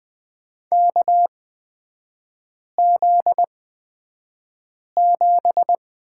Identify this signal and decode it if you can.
Morse code, heard as a keyed tone, about 20 wpm, spaced as for 5 wpm, 700 Hz. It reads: KZ7
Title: 243 Lesson 39e - Letter z - Three Characters 20wpm